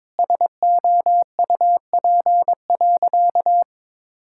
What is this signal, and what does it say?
Morse code, heard as a keyed tone, about 22 wpm, 700 Hz. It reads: SOUP.
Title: Sentences from Top 1000 Words - Review Part 6 - 22wpm